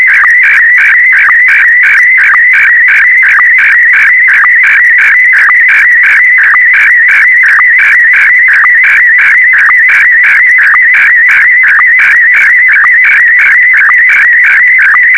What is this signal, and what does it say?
SSTV (slow-scan television), Scottie DX.
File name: sstv / SDX-VK4MSL